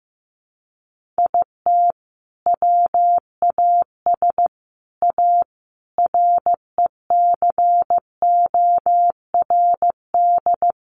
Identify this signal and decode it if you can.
Morse code, heard as a keyed tone, about 15 wpm, 700 Hz. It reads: IT WAS A RECORD
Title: Sentences from Top 500 Words 15wpm